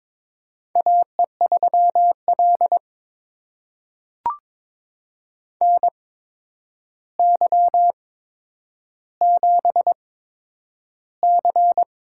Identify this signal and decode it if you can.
Morse code, heard as a keyed tone, about 23 wpm, spaced as for 10 wpm, 700 Hz. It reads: AE3L NY7C
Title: USA Call Signs - 4 Characters - 8x Character Spacing - 22wpm